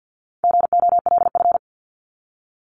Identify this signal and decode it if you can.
Morse code, heard as a keyed tone, about 50 wpm, 700 Hz. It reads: GOLF
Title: Top 1751-2000 Words 50wpm